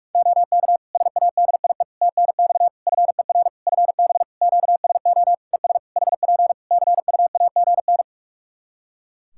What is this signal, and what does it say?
Morse code, heard as a keyed tone, about 46 wpm, 700 Hz. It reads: OK SADIE TNX FER FB QSO ES HP CUAGN
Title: Practice QSO - Exchange 4 - Ending - 45wpm